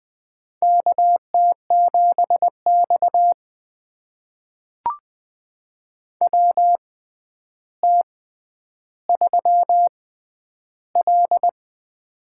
Morse code, heard as a keyed tone, about 20 wpm, spaced as for 10 wpm, 700 Hz. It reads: KT7X WT3L
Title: USA Call Signs - 4 Characters - 6x Character Spacing - 20wpm